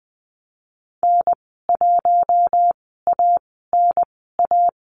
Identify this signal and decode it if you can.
Morse code, heard as a keyed tone, about 20 wpm, spaced as for 15 wpm, 700 Hz. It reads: N1ANA
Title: 076 Lesson 13g - Letter L - Call Signs 20wpm